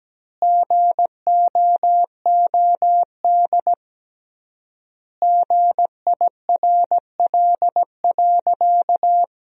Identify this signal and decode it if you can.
Morse code, heard as a keyed tone, about 17 wpm, 700 Hz. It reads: GOOD GIRL.
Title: 3x Word-Spacing - Sentences from Top 400 Words - 17wpm